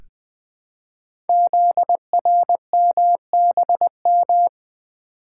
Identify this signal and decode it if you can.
Morse code, heard as a keyed tone, about 20 wpm, 700 Hz. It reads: ZRMBM
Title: Code Groups - 5 Letters - 20wpm